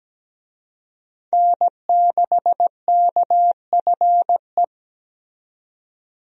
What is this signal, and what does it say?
Morse code, heard as a keyed tone, about 17 wpm, 700 Hz. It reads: N6KFE
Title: USA Call Signs - 5 Characters - 17wpm